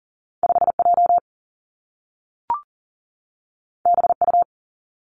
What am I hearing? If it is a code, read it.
Morse code, heard as a keyed tone, about 42 wpm, 700 Hz. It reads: 5 J B U
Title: Two Challenging Letter-Number Combinations 40wpm